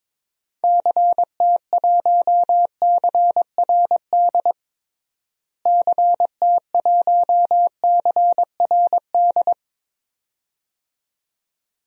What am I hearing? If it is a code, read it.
Morse code, heard as a keyed tone, about 21 wpm, 700 Hz. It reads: CT1CRD CT1CRD
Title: DXCC-100 Call Signs - 6 Characters - Repeated 2x - 22wpm